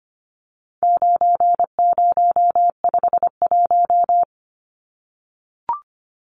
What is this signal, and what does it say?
Morse code, heard as a keyed tone, about 25 wpm, 700 Hz. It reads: 9051
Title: Numbers - 4-Digit 25wpm